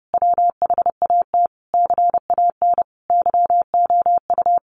Morse code, heard as a keyed tone, about 28 wpm, 700 Hz. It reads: WHAT CAN YOU
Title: Sentences from Top 800 Words 30wpm